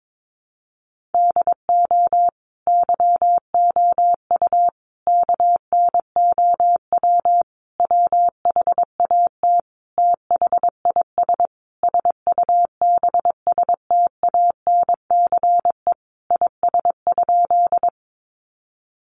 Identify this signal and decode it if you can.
Morse code, heard as a keyed tone, about 22 wpm, 700 Hz. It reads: DO YOU KNOW WHAT THIS SUBSTANCE IS?
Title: Sentences from Top 1500 Words 22wpm